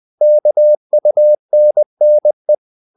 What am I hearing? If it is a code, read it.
Morse code, heard as a keyed tone, about 20 wpm, 600 Hz. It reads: KUNNE
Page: Ferie 20 wpm